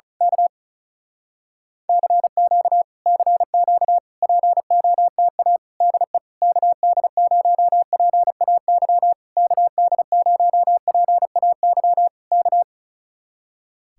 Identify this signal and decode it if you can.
Morse code, heard as a keyed tone, about 34 wpm, 700 Hz. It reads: K CQ CQ POTA DE KD0PAY KD0PAY K
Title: Speed-Racing Common QSO Phrases Encore - 35wpm